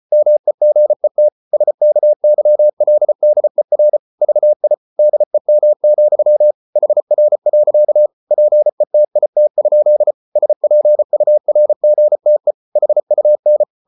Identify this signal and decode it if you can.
Morse code, heard as a keyed tone, about 34 wpm, 600 Hz. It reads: MEGET SKYLDER VI DEM, HR. PETIT? SPURGTE HUN
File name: Ferie QRQ 34wpm